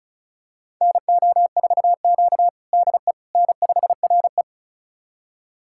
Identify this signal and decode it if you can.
Morse code, heard as a keyed tone, about 36 wpm, 700 Hz. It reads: NO4Q DE N5RE
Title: USA Callsign DE Callsign - 4 Characters - 35wpm